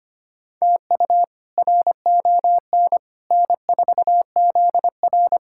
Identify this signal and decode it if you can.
Morse code, heard as a keyed tone, about 25 wpm, 700 Hz. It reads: TU RON N4ZR
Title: K1USN SST QSO Practice - 25wpm